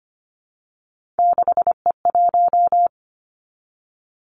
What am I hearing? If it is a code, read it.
Morse code, heard as a keyed tone, about 25 wpm, 700 Hz. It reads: 6E1
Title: Three Challenging Letter-Number Combinations 25wpm